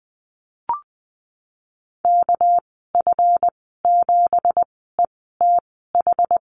Morse code, heard as a keyed tone, about 20 wpm, spaced as for 14 wpm, 700 Hz. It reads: KF7ETH
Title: 231 Lesson 37g - Letter x - Call Signs 20wpm